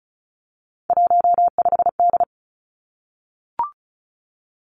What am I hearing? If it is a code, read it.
Morse code, heard as a keyed tone, about 36 wpm, 700 Hz. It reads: 15D
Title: Three Challenging Letter-Number Combinations 35wpm